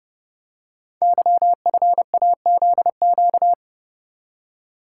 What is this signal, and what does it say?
Morse code, heard as a keyed tone, about 30 wpm, 700 Hz. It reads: YFAZQ
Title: Code Groups - 5 Letter:Numbers - 30wpm